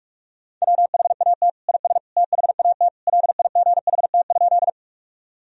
Morse code, heard as a keyed tone, about 46 wpm, 700 Hz. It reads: WHAT IS THAT LIGHT?
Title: Sentences from Top 1000 Words - Review Part 1 - 45wpm